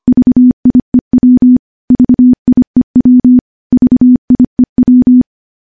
Morse code, heard as a keyed tone, about 25 wpm, 250 Hz. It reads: VIEW VIEW VIEW
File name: morse / morse-training-25WPM